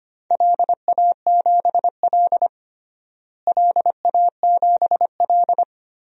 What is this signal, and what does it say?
Morse code, heard as a keyed tone, about 25 wpm, 700 Hz. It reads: LA7L LA7L
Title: DXCC-100 Call Signs - 4 Characters - Repeated 2x - 25wpm